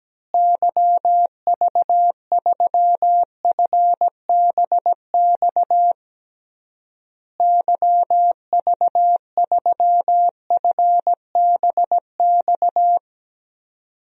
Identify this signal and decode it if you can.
Morse code, heard as a keyed tone, about 17 wpm, 700 Hz. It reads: YV3FBX YV3FBX
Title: DXCC-100 Call Signs - 6 Characters - Repeated 3x - 17wpm